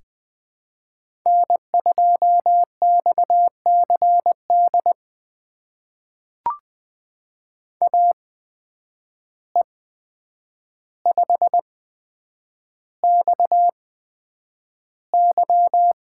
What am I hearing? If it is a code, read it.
Morse code, heard as a keyed tone, about 20 wpm, spaced as for 9 wpm, 700 Hz. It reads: N2XCD AE5XY
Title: USA Call Signs - 5 Characters - 8x Character Spacing - 20wpm